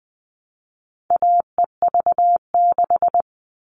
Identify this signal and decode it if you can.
Morse code, heard as a keyed tone, about 20 wpm, 700 Hz. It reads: AEV6
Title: All Cut Numbers - 4-Digit 20wpm